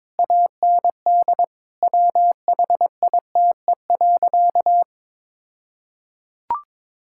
Morse code, heard as a keyed tone, about 22 wpm, 690 Hz. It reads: AND WHITE.
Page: morse-code-ninja-practice-22wpm